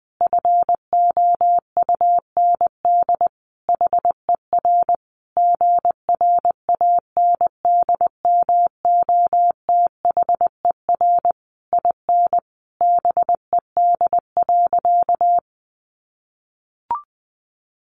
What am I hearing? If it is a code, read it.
Morse code, heard as a keyed tone, about 19 wpm, 700 Hz. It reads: FOUND HER GRANDMOTHER IN BED.
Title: Sentences from Top 1200 Words 20wpm